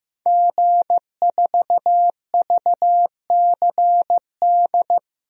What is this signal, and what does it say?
Morse code, heard as a keyed tone, about 15 wpm, 700 Hz. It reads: G4VCD
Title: DXCC-100 Call Signs - 5 Characters - Repeated 2x - 15wpm